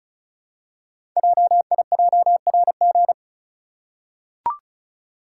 Morse code, heard as a keyed tone, about 36 wpm, 690 Hz. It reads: JIJRG
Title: Code Groups - 5 Letters - 35wpm